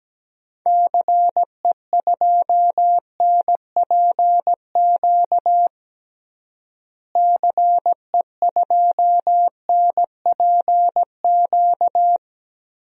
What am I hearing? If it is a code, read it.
Morse code, heard as a keyed tone, about 17 wpm, 700 Hz. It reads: CE2NPQ CE2NPQ
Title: DXCC-100 Call Signs - 6 Characters - Repeated 3x - 17wpm